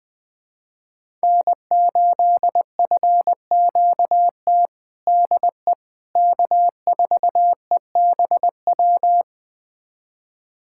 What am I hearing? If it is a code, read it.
Morse code, heard as a keyed tone, about 20 wpm, 700 Hz. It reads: N8FQT DE K4EBW
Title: USA Callsign DE Callsign - 5 Characters - 20wpm